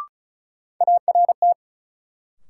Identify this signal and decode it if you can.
Morse code, heard as a keyed tone, about 36 wpm, 700 Hz. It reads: ART
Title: Sentences from Top 600 Words - New Word Drill - 35wpm